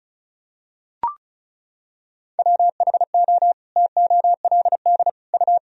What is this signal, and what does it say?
Morse code, heard as a keyed tone, about 36 wpm, 700 Hz. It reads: WHO TOLD U
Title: Sentences from Top 400 Words and Abbreviations 35wpm